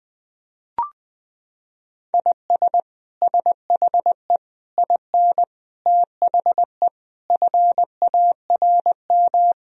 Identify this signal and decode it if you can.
Morse code, heard as a keyed tone, about 21 wpm, 700 Hz. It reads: IS SHE IN THE FARM
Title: Sentences from Top 1000 Words - Review Part 2 - 20wpm